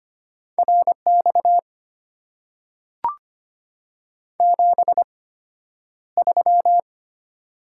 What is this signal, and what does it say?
Morse code, heard as a keyed tone, about 25 wpm, spaced as for 11 wpm, 690 Hz. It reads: RX 73
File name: Top 90 QSO Elements - 8x Character Spacing - 25wpm